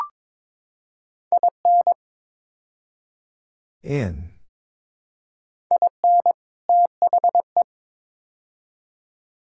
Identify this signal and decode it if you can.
Morse code, heard as a keyed tone, about 21 wpm, 700 Hz. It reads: IN IN THE
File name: practice / 22wpm / Build A Phrase - 5-Word Phrases - 22wpm